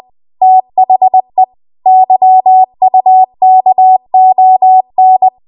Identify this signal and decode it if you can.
Morse code, heard as a keyed tone, about 20 wpm, 750 Hz. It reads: THE YUKON
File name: Archive / 20 WPM / 20WPM